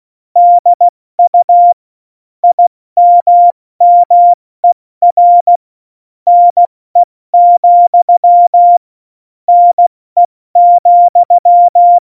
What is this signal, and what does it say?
Morse code, heard as a keyed tone, about 16 wpm, spaced as for 14 wpm, 700 Hz. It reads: DU IMMER NE, NE,